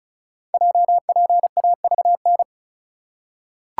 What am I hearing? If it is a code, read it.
Morse code, heard as a keyed tone, about 33 wpm, 690 Hz. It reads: JPAVN
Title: Code Groups - 5 Letters - 35wpm